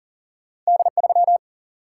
Morse code, heard as a keyed tone, about 40 wpm, 700 Hz. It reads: D3
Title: Single-Letter Single-Number 40wpm